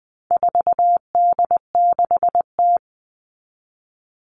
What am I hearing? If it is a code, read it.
Morse code, heard as a keyed tone, about 20 wpm, 700 Hz. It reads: 4D6T
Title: All Cut Numbers - 4-Digit 20wpm